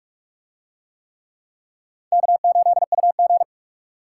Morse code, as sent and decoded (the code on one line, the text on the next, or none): -.- ---.. ..- --.
K8UG